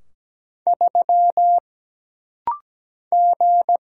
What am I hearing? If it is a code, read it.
Morse code, heard as a keyed tone, about 17 wpm, 700 Hz. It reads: 3 G
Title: Single Letter-Number - Rapid-Fire 17wpm